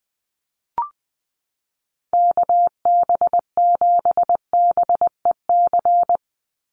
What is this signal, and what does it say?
Morse code, heard as a keyed tone, about 19 wpm, 700 Hz. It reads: KB7BEC
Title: USA Call Signs 20wpm